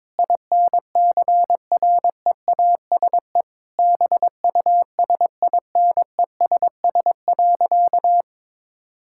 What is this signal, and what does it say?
Morse code, heard as a keyed tone, about 22 wpm, 700 Hz. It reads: INCREASE BUSINESS.
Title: Sentences from Top 1000 Words - Review Part 6 - 22wpm